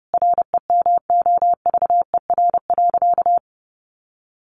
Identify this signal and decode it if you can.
Morse code, heard as a keyed tone, about 30 wpm, 700 Hz. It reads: REMOVER.
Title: Additional Easy Phrases 30wpm